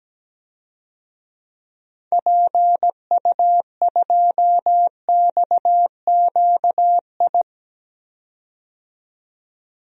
Morse code, heard as a keyed tone, about 17 wpm, 700 Hz. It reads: PU2XQI
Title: DXCC-100 Call Signs - 6 Characters - 17wpm